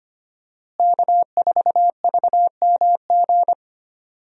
Morse code, heard as a keyed tone, about 25 wpm, 700 Hz. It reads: K4VMG
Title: USA Call Signs - 5 Characters - Repeated 2x - 25wpm